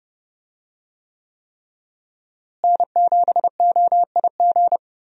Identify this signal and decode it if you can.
Morse code, heard as a keyed tone, about 30 wpm, 700 Hz. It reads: N7OIG
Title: USA Call Signs - 5 Characters - Repeated 3x - 30wpm